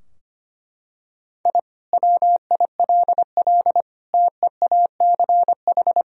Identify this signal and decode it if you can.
Morse code, heard as a keyed tone, about 25 wpm, 700 Hz. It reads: I WILL TEACH
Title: Sentences from Top 1000 Words - Review Part 3 - 25wpm